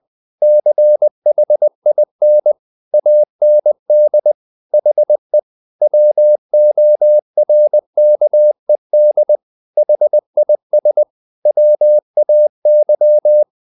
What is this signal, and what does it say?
Morse code, heard as a keyed tone, about 20 wpm, 600 Hz. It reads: CHIN AND HE WORKED HIS WAY
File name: the old man and the sea / the old man and the sea part 1 20wpm